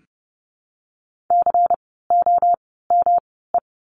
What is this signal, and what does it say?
Morse code, heard as a keyed tone, about 30 wpm, spaced as for 17 wpm, 700 Hz. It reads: COME
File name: course / 30:10wpm / 108 Lesson 18f - Letter m - Words 30wpm